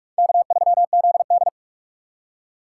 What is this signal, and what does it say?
Morse code, heard as a keyed tone, about 56 wpm, 700 Hz. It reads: K3ZD